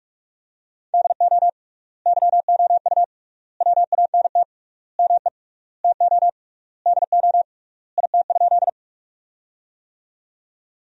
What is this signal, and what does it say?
Morse code, heard as a keyed tone, about 46 wpm, 700 Hz. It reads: DO YOU WANT ME TO DO IT?
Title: 3x Word-Spacing - Sentences from Top 200 Words - 45wpm